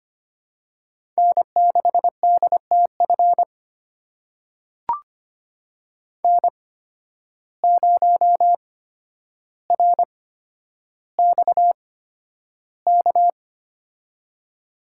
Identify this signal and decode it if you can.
Morse code, heard as a keyed tone, about 25 wpm, spaced as for 11 wpm, 700 Hz. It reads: N6DTF N0RXK
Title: USA Call Signs - 8x Character Spacing - 25wpm